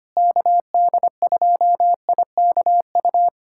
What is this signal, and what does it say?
Morse code, heard as a keyed tone, about 25 wpm, 700 Hz. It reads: KD2IKU
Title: USA Callsign DE Callsign 25wpm